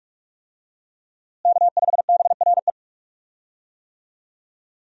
Morse code, heard as a keyed tone, about 42 wpm, 700 Hz. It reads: K5BRE
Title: USA Call Signs - 5 Characters - Repeated 3x - 45wpm